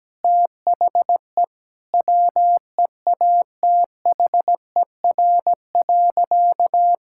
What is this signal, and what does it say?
Morse code, heard as a keyed tone, about 17 wpm, 700 Hz. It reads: THE WEATHER.
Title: Sentences from Top 1000 Words - Review Part 5 - 17wpm